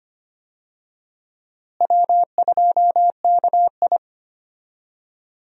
Morse code, heard as a keyed tone, about 25 wpm, 700 Hz. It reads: W2KI
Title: USA Call Signs - 4 Characters - Repeated 2x - 25wpm